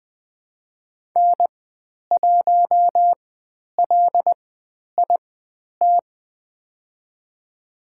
Morse code, heard as a keyed tone, about 20 wpm, spaced as for 10 wpm, 700 Hz. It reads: N1LIT
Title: 095 Lesson 16g - Letter c - Call Signs 20wpm